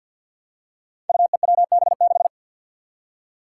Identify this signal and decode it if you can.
Morse code, heard as a keyed tone, about 52 wpm, 700 Hz. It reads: UEWB6
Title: Five Challenging Letter-Number Combinations 50wpm